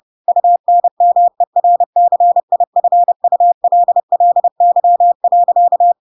Morse code, heard as a keyed tone, about 30 wpm, 700 Hz. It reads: UNMERCIFULLY.
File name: Pinocchio / Pinocchio ch1 30 30 ik1hge